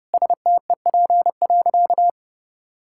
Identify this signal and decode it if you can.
Morse code, heard as a keyed tone, about 30 wpm, 700 Hz. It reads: STEP.
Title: Sentences from Top 1600 Words 30wpm